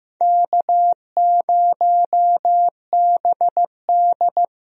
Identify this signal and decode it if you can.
Morse code, heard as a keyed tone, about 16 wpm, 700 Hz. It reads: K0BD
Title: USA Call Signs - 4 Characters - Repeated 2x - 15wpm